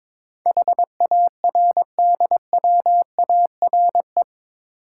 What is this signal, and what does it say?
Morse code, heard as a keyed tone, about 22 wpm, 700 Hz. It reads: HARDWARE
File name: practice / 22wpm / Eight Letter Words - 22wpm